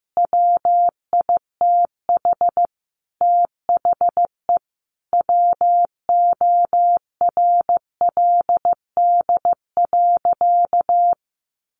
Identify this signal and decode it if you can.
Morse code, heard as a keyed tone, about 14 wpm, 700 Hz. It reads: WITH THE WORLD.
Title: Sentences from Top 200 Words 15wpm